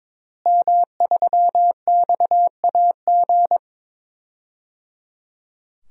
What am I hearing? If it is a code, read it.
Morse code, heard as a keyed tone, about 22 wpm, 700 Hz. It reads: M3XAG
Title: DXCC-100 Call Signs - 5 Characters - Repeated 3x - 22wpm